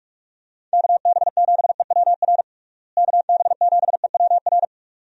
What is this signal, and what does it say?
Morse code, heard as a keyed tone, about 42 wpm, 700 Hz. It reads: KB7EWR KB7EWR
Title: USA Call Signs - 6 Characters - Repeated 2x - 45wpm